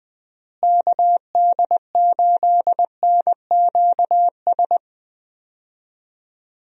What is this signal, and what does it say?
Morse code, heard as a keyed tone, about 20 wpm, 700 Hz. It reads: KD8NQS
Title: USA Call Signs - 6 Characters - Repeated 3x - 20wpm